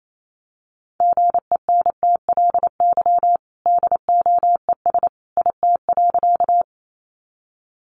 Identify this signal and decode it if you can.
Morse code, heard as a keyed tone, about 28 wpm, 700 Hz. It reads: GENTLY DOES IT.
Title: Sentences from Top 1500 Words 28wpm